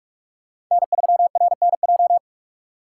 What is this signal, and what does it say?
Morse code, heard as a keyed tone, about 46 wpm, 700 Hz. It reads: N3RNJ
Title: USA Call Signs - 5 Characters - 6x Character Spacing - 45wpm